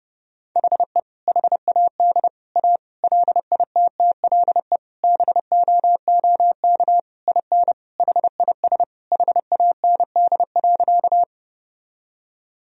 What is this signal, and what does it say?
Morse code, heard as a keyed tone, about 30 wpm, 700 Hz. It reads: HE HAD A LITTLE BOOK IN HIS HAND.
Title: Sentences from Top 1000 Words - Review Part 1 - 30wpm